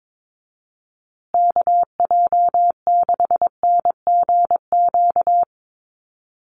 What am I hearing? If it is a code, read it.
Morse code, heard as a keyed tone, about 22 wpm, 700 Hz. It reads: KJ6NGQ
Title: USA Call Signs Repeated 2x 22wpm